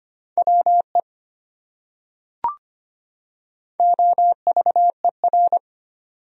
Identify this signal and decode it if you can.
Morse code, heard as a keyed tone, about 25 wpm, 700 Hz. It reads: WE OVER
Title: Top 100 Words 25wpm